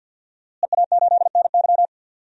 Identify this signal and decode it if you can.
Morse code, heard as a keyed tone, about 48 wpm, 700 Hz. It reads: EA8NY
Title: DXCC-100 Call Signs - 5 Characters - Repeated 2x - 50wpm